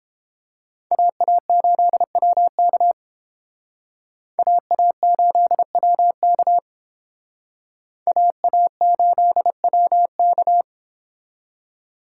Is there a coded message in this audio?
Yes: Morse code from 0.9 to 10.6 s, heard as a keyed tone, about 31 wpm, 700 Hz. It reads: AA8WK AA8WK AA8WK